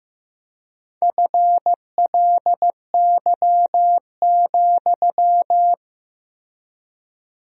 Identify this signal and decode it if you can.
Morse code, heard as a keyed tone, about 15 wpm, 700 Hz. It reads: FLY,
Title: 3x Word-Spacing - Sentences from Top 500 Words - 15wpm